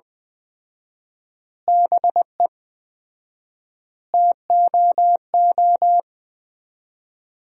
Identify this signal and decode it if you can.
Morse code, heard as a keyed tone, about 20 wpm, 700 Hz. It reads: BE TOO
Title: 4x Word-Spacing - Sentences from Top 300 Words - 20wpm